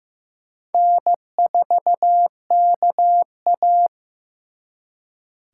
Morse code, heard as a keyed tone, about 18 wpm, 700 Hz. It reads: N4KA
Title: Speed Racing - USA Call Signs 15wpm